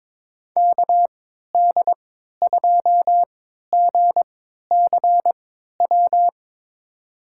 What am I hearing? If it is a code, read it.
Morse code, heard as a keyed tone, about 22 wpm, spaced as for 13 wpm, 700 Hz. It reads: KD2GCW
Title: USA Call Signs - 6 Characters - 3x Character Spacing - 22wpm